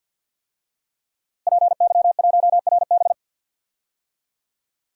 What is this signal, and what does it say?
Morse code, heard as a keyed tone, about 53 wpm, 700 Hz. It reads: PY1RB
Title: DXCC-100 Call Signs - 5 Characters - Repeated 2x - 50wpm